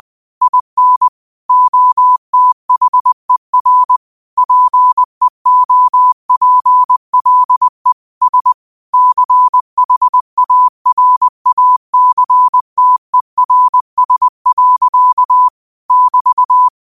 Morse code, heard as a keyed tone, about 20 wpm, 1000 Hz. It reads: IN OTHER PEOPLE S CHARACTERS. <BT>
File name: qotdfiles / 20wpm